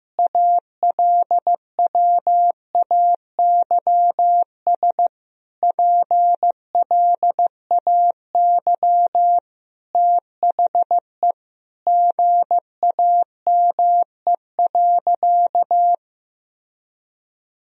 Morse code, heard as a keyed tone, about 16 wpm, 700 Hz. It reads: ALWAYS PLAY THE GAME.
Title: Sentences from Top 1000 Words - Review Part 3 - 15wpm